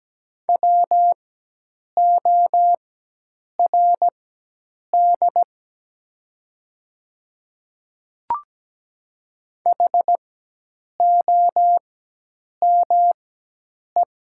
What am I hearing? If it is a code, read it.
Morse code, heard as a keyed tone, about 17 wpm, spaced as for 8 wpm, 700 Hz. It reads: WORD HOME
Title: 4x Character Spacing - Top 200 Words - 17wpm